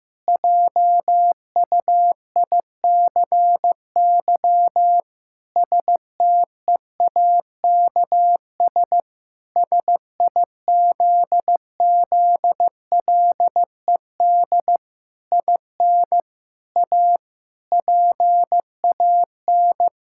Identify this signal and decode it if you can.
Morse code, heard as a keyed tone, about 15 wpm, 700 Hz. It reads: JUICY STEAKS SIZZLED IN A PAN
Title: Pangram Challenge v2 - 15wpm